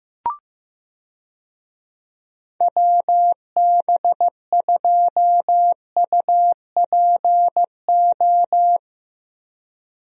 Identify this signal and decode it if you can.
Morse code, heard as a keyed tone, about 15 wpm, 700 Hz. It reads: WB2UPO
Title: USA Call Signs - 6 Characters - Repeated 3x - 15wpm